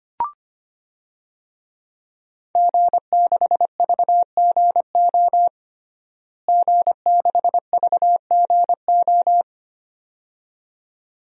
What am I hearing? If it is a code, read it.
Morse code, heard as a keyed tone, about 25 wpm, 700 Hz. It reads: G6VGO G6VGO
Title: DXCC-100 Call Signs - 5 Characters - Repeated 2x - 25wpm